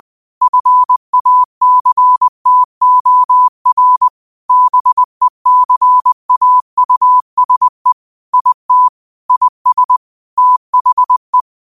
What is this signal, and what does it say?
Morse code, heard as a keyed tone, about 20 wpm, 1000 Hz. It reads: FACTOR BECAUSE IT IS THE
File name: qotdfiles / 20wpm